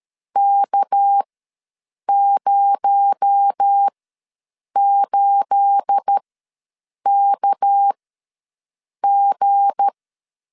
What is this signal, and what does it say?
Morse code, heard as a keyed tone, about 13 wpm, spaced as for 7 wpm, 800 Hz. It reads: K08KG
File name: LEZIONI / 9 28 ws 12wpm 3354